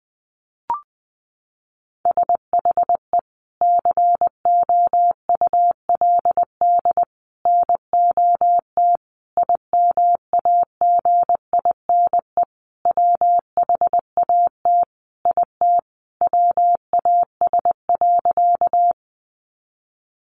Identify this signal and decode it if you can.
Morse code, heard as a keyed tone, about 20 wpm, 700 Hz. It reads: SHE COULD NOT IMAGINE WHAT IT WAS.
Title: Sentences from Top 1400 Words 20wpm